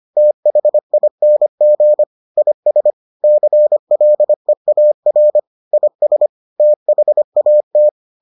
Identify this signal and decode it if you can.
Morse code, heard as a keyed tone, about 25 wpm, 600 Hz. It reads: THING IS CLEAR IS THAT